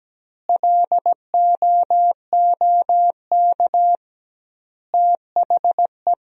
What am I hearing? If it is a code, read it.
Morse code, heard as a keyed tone, about 17 wpm, 700 Hz. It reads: LOOK THE
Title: 2x Word-Spacing - Sentences from Top 100 Words - 17wpm